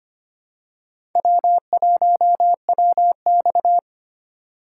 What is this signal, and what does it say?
Morse code, heard as a keyed tone, about 25 wpm, 690 Hz. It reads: W1WX
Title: USA Call Signs - 4 Characters - 4x Character Spacing - 25wpm